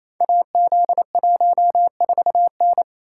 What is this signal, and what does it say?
Morse code, heard as a keyed tone, about 27 wpm, 700 Hz. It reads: AZ14N
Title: Code Groups - 5 Letter:Numbers - 28wpm